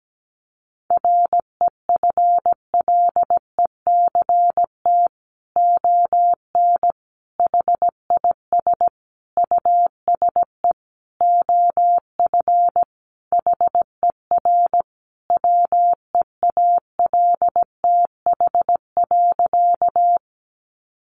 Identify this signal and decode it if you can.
Morse code, heard as a keyed tone, about 17 wpm, 700 Hz. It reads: REFLECT ON HIS USE OF HER WEALTH.
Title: Sentences from Top 1500 Words 17wpm